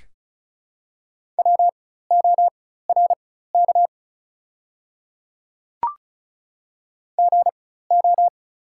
Morse code, heard as a keyed tone, about 34 wpm, spaced as for 16 wpm, 700 Hz. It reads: WORK GO
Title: 4x Character Spacing - Top 200 Words - 35wpm